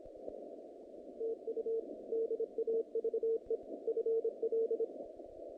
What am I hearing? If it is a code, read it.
Morse code, heard as a keyed tone, about 26 wpm, 450 Hz. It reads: TU DAVE FL
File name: contests / naqp-cw-k5tr-op-k5pi- -2200